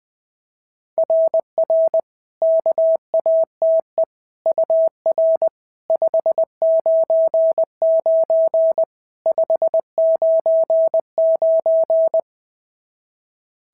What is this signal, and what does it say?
Morse code, heard as a keyed tone, about 19 wpm, 650 Hz. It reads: RR KATE UR 599 599
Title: SOTA QSO Practice 20wpm